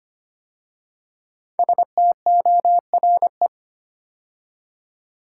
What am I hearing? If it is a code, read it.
Morse code, heard as a keyed tone, about 25 wpm, 700 Hz. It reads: STORE
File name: 5 Letter Words - First 3 Letters Given - 25wpm